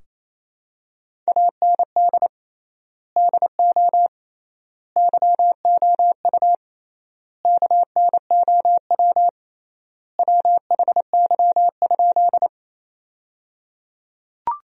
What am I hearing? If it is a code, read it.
Morse code, heard as a keyed tone, about 28 wpm, 700 Hz. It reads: AND DO YOU KNOW WHY?
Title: 3x Word-Spacing - Sentences from Top 200 Words - 28wpm